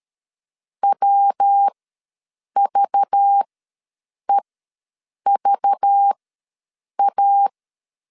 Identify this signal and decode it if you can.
Morse code, heard as a keyed tone, about 13 wpm, spaced as for 7 wpm, 800 Hz. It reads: WVEVA
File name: LEZIONI / 6 PKX0 / 6 PK ws 12wpm 3116